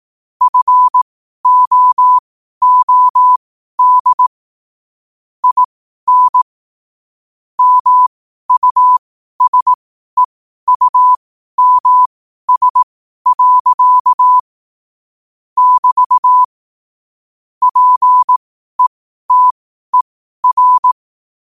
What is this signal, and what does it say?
Morse code, heard as a keyed tone, about 18 wpm, spaced as for 12 wpm, 1000 Hz. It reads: FOOD IN MUSEUMS. <BT> PETER
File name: qotdfiles / 12wpm